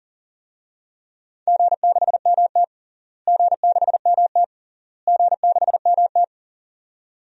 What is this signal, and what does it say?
Morse code, heard as a keyed tone, about 37 wpm, 700 Hz. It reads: G6MT G6MT G6MT
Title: DXCC-100 Call Signs - 4 Characters - Repeated 3x - 40wpm